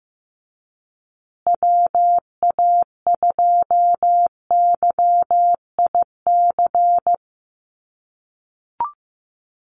Morse code, heard as a keyed tone, about 15 wpm, 700 Hz. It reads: WA2YIC
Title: USA Call Signs Repeated 3x 15wpm